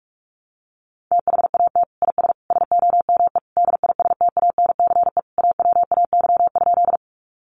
Morse code, heard as a keyed tone, about 44 wpm, 700 Hz. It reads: THAT IS SOME DISTANCE AWAY?
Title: Sentences from Top 1200 Words 45wpm